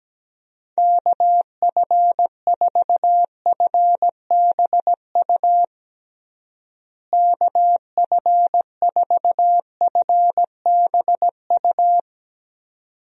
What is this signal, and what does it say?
Morse code, heard as a keyed tone, about 17 wpm, 700 Hz. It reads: KF4FBU KF4FBU
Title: USA Call Signs - 6 Characters - Repeated 2x - 17wpm